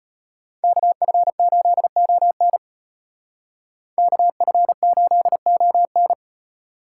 Morse code, heard as a keyed tone, about 31 wpm, 700 Hz. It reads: KF8ON KF8ON
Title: Speed Racing - USA 2x2 Call Signs 25wpm